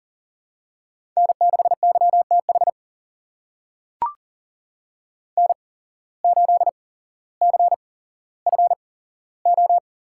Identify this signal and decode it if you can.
Morse code, heard as a keyed tone, about 38 wpm, spaced as for 17 wpm, 700 Hz. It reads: N6YTH N8CFO
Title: USA Call Signs - 5 Characters - 8x Character Spacing - 40wpm